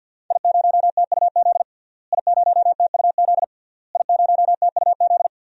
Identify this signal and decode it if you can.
Morse code, heard as a keyed tone, about 51 wpm, 700 Hz. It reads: I0TUZ I0TUZ I0TUZ
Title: DXCC-100 Call Signs - 5 Characters - Repeated 3x - 50wpm